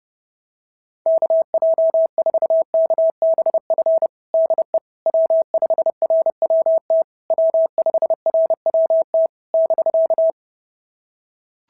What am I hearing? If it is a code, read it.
Morse code, heard as a keyed tone, about 31 wpm, 690 Hz. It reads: KJ4KBF DE W5RWT W5RWT <BK>
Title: SOTA QSO Practice 30wpm